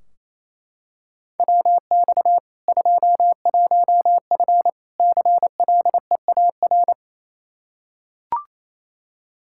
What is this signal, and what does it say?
Morse code, heard as a keyed tone, about 28 wpm, 700 Hz. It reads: WX 21F CLEAR
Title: Practice QSO - Exchange 2 - Info - 28wpm